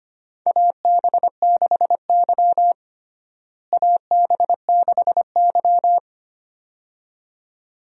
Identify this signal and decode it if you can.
Morse code, heard as a keyed tone, about 25 wpm, 700 Hz. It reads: AB6Y AB6Y
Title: USA Call Signs - 4 Characters - Repeated 2x - 25wpm